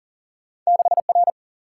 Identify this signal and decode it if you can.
Morse code, heard as a keyed tone, about 40 wpm, 700 Hz. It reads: 6R
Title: Single-Letter Single-Number 40wpm